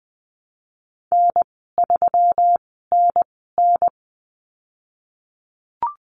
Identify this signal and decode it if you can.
Morse code, heard as a keyed tone, about 20 wpm, spaced as for 15 wpm, 700 Hz. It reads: N3NN
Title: 186 Lesson 30g - Letter b - Call Signs 20wpm